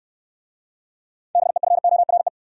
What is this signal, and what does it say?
Morse code, heard as a keyed tone, about 64 wpm, 700 Hz. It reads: BEFORE